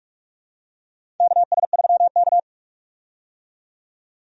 Morse code, heard as a keyed tone, about 41 wpm, 700 Hz. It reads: KS3K
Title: USA Call Signs - 4 Characters - 45wpm